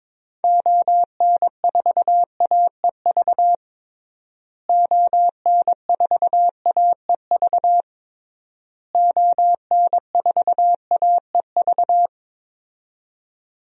Morse code, heard as a keyed tone, about 22 wpm, 700 Hz. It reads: ON4AEV ON4AEV ON4AEV
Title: DXCC-100 Call Signs - 6 Characters - Repeated 3x - 22wpm